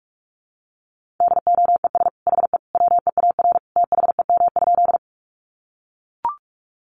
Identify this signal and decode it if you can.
Morse code, heard as a keyed tone, about 44 wpm, 700 Hz. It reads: DOES HE WEAR THEM?
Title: Sentences from Top 1500 Words 45wpm